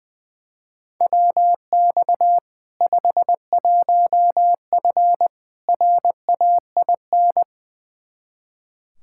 Morse code, heard as a keyed tone, about 20 wpm, 700 Hz. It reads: WX 51F RAIN
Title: K1USN SST QSO Practice 20wpm